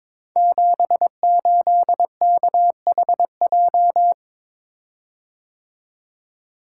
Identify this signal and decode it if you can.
Morse code, heard as a keyed tone, about 23 wpm, 700 Hz. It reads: 78KHJ
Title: Code Groups - 5 Letter:Numbers - 22wpm